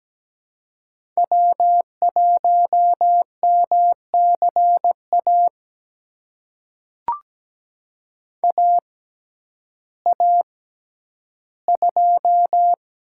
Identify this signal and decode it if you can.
Morse code, heard as a keyed tone, about 17 wpm, spaced as for 10 wpm, 700 Hz. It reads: W1MCA AA2
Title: USA Call Signs - 5 Characters - 6x Character Spacing - 17wpm